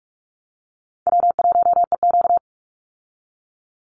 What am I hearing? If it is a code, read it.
Morse code, heard as a keyed tone, about 44 wpm, 700 Hz. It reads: W1EQ
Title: Easy Call Signs Repeated 2x - 45wpm